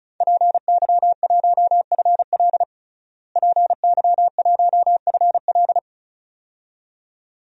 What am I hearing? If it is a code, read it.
Morse code, heard as a keyed tone, about 36 wpm, 700 Hz. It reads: PY1FL PY1FL
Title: DXCC-100 Call Signs - 5 Characters - Repeated 2x - 35wpm